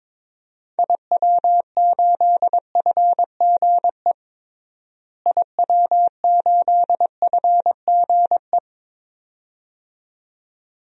Morse code, heard as a keyed tone, about 21 wpm, 700 Hz. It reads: IW8FGE IW8FGE
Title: DXCC-100 Call Signs - 6 Characters - Repeated 2x - 22wpm